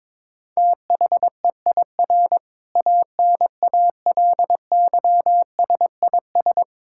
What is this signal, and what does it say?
Morse code, heard as a keyed tone, about 22 wpm, 700 Hz. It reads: THEIR ANALYSIS
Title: Sentences from Top 1000 Words - Review Part 5 - 22wpm